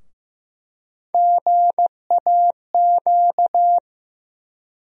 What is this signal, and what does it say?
Morse code, heard as a keyed tone, about 15 wpm, 700 Hz. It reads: GAQ
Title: USA Call Sign Postfixes - 3 Letters 15wpm